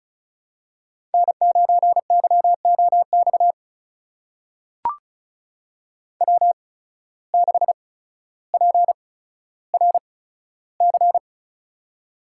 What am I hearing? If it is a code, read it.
Morse code, heard as a keyed tone, about 34 wpm, spaced as for 15 wpm, 700 Hz. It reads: N9YOX W6PRC
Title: USA Call Signs - 5 Characters - 8x Character Spacing - 35wpm